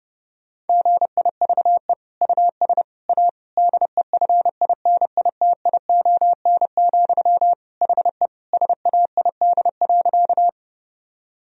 Morse code, heard as a keyed tone, about 30 wpm, 700 Hz. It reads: GIVE US A DEFINITION, HE SAID.
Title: Sentences from Top 1000 Words - Review Part 4 - 30wpm